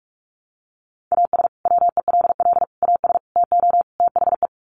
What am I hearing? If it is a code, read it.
Morse code, heard as a keyed tone, about 44 wpm, 700 Hz. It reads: AS WELL AS TO THE
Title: Sets of 5 Words Encore2 45wpm